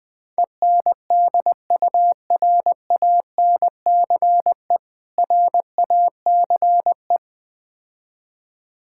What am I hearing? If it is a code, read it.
Morse code, heard as a keyed tone, about 20 wpm, 700 Hz. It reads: ENDURANCE RACE
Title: Difficult Words - Speed-Racing with Hints - 20wpm